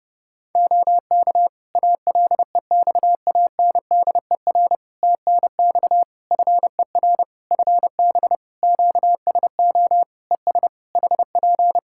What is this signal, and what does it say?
Morse code, heard as a keyed tone, about 30 wpm, 700 Hz. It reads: OK ALEXANDER TNX FER FB QSO ES HP
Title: K1USN SST QSO Practice 30wpm